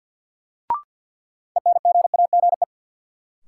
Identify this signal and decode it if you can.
Morse code, heard as a keyed tone, about 53 wpm, 700 Hz. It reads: ENGAGE
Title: Sentences from Top 1500 Words - New Word Drill - 50wpm